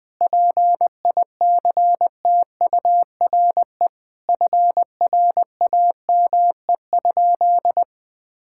Morse code, heard as a keyed tone, about 20 wpm, 700 Hz. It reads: PICTURE FRAME?
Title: Sentences from Top 1000 Words - Review Part 6 - 20wpm